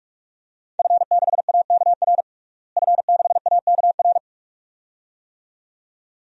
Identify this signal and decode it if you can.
Morse code, heard as a keyed tone, about 44 wpm, 700 Hz. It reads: F6AKR F6AKR
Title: DXCC-100 Call Signs - 5 Characters - Repeated 2x - 45wpm